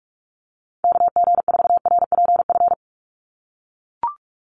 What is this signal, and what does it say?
Morse code, heard as a keyed tone, about 44 wpm, 700 Hz. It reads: KG4RPF
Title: USA Call Signs 45wpm